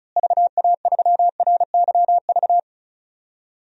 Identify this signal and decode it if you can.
Morse code, heard as a keyed tone, about 36 wpm, 700 Hz. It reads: VA3RYV
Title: DXCC-100 Call Signs - 6 Characters - Repeated 2x - 35wpm